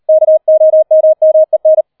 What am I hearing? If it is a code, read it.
Morse code, heard as a keyed tone, about 38 wpm, 630 Hz. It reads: KOMMEN
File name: 40wpm